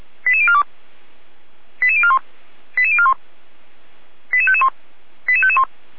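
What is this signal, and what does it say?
Five-tone selective call (ZVEI), heard as quick runs of five tones, tones of about 70 ms. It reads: 88042 88041 88041 88551 88551
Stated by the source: ZVEI five-tone call 88352 received on an SDR.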